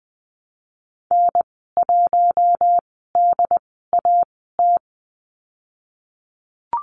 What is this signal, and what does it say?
Morse code, heard as a keyed tone, about 19 wpm, spaced as for 14 wpm, 700 Hz. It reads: N1DAT
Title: 140 Lesson 23g - Letter f - Call Signs 20wpm